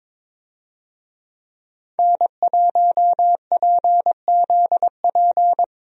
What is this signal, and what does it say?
Morse code, heard as a keyed tone, about 23 wpm, 700 Hz. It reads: N1PZP
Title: USA Call Signs - 5 Characters - 22wpm